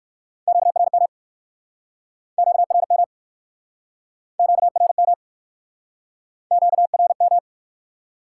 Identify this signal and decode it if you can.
Morse code, heard as a keyed tone, about 58 wpm, 700 Hz. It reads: QRM QRM QRM QRM